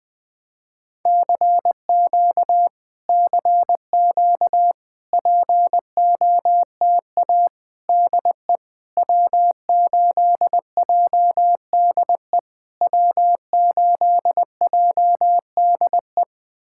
Morse code, heard as a keyed tone, about 20 wpm, 700 Hz. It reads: CQ CQ POTA DE W8JDE W8JDE
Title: POTA QSO Practice - 20wpm